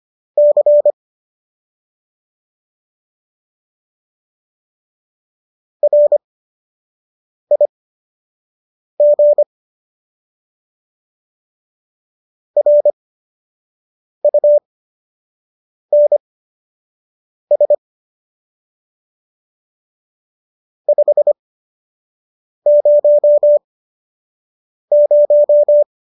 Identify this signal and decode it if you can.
Morse code, heard as a keyed tone, about 25 wpm, spaced as for 6 wpm, 590 Hz. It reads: C RIG RUNS 500
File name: qso002 06